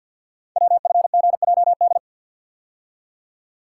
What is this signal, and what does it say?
Morse code, heard as a keyed tone, about 49 wpm, 700 Hz. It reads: WFGJD
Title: Code Groups - 5 Letter:Numbers - 50wpm